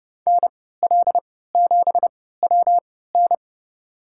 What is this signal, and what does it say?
Morse code, heard as a keyed tone, about 30 wpm, spaced as for 17 wpm, 700 Hz. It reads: NL7WN
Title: USA Call Signs - 3x Character Spacing - 30wpm